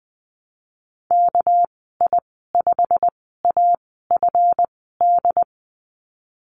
Morse code, heard as a keyed tone, about 20 wpm, spaced as for 15 wpm, 700 Hz. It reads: KI5AFD
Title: 200 Lesson 32g - Letter k - Call Signs 20wpm